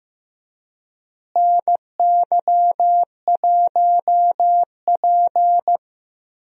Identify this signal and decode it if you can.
Morse code, heard as a keyed tone, about 15 wpm, 700 Hz. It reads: NY1P
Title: USA Call Signs - 4 Characters - Repeated 2x - 15wpm